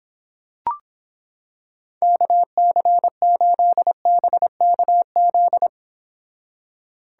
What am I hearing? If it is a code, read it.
Morse code, heard as a keyed tone, about 26 wpm, 700 Hz. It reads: KC8BKZ